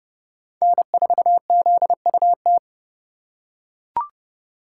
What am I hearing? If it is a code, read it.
Morse code, heard as a keyed tone, about 30 wpm, 700 Hz. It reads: N4ZUT
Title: USA Call Signs - 5 Characters - 2x Character Spacing - 30wpm